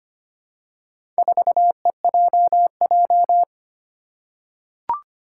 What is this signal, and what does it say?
Morse code, heard as a keyed tone, about 25 wpm, 700 Hz. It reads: 4EJJ
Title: Four Challenging Letter-Number Combinations 25wpm